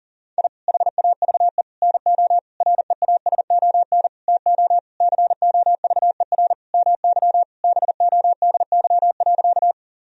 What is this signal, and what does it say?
Morse code, heard as a keyed tone, about 40 wpm, 700 Hz. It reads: I HAVE NO REASON TO COVER MY BODY.
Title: Sentences from Top 1000 Words - Review Part 1 - 40wpm